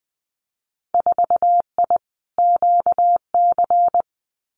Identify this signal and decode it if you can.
Morse code, heard as a keyed tone, about 19 wpm, 700 Hz. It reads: 4I QC
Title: Winter Field Day Exchanges Encore 20wpm